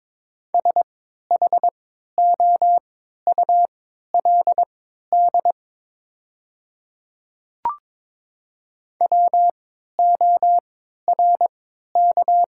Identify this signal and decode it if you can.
Morse code, heard as a keyed tone, about 22 wpm, spaced as for 13 wpm, 700 Hz. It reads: SHOULD WORK
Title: 3x Character Spacing - Top 300 Words - 22wpm